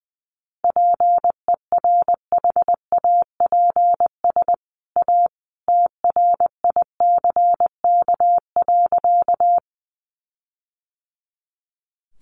Morse code, heard as a keyed tone, about 20 wpm, 700 Hz. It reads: PERHAPS A TRICK.
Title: Sentences from Top 1500 Words 20wpm